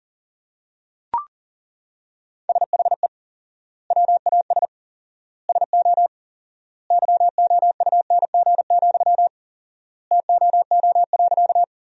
Morse code, heard as a keyed tone, about 37 wpm, 700 Hz. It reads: SHE WAS SO YOUNG, TOO.
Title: 4x Word-Spacing - Sentences from Top 400 Words - 40wpm